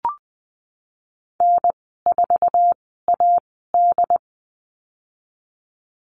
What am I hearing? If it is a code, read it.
Morse code, heard as a keyed tone, about 20 wpm, spaced as for 15 wpm, 700 Hz. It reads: N4AD
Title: 082 Lesson 14g - Number 2 - Call Signs 20wpm